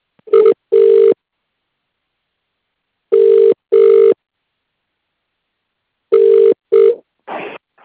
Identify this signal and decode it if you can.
Ringback tone (UK), 400 + 450 Hz.